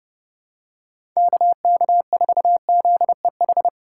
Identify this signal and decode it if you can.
Morse code, heard as a keyed tone, about 30 wpm, 700 Hz. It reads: KK4ZEH
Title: USA Call Signs - 6 Characters - 2x Character Spacing - 30wpm